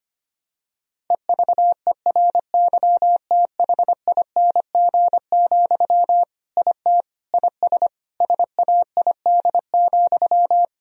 Morse code, heard as a keyed tone, about 25 wpm, 700 Hz. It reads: EVERYTHING, IT IS SAID,
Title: Sentences from Top 1000 Words - Review Part 1 - 25wpm